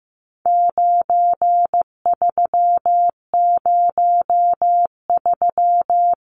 Morse code, heard as a keyed tone, about 15 wpm, 700 Hz. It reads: 9303
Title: Numbers - 4-Digit 15wpm